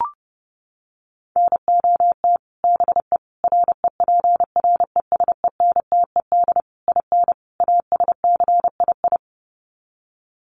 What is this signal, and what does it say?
Morse code, heard as a keyed tone, about 30 wpm, 700 Hz. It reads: NOT BE REPRESENTED IN ASCII
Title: Sets of 5 Words 30wpm